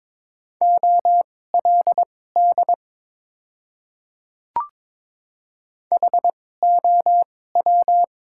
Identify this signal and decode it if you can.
Morse code, heard as a keyed tone, about 22 wpm, spaced as for 16 wpm, 700 Hz. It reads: OLD HOW
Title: 2x Character Spacing - Top 300 Words - 22wpm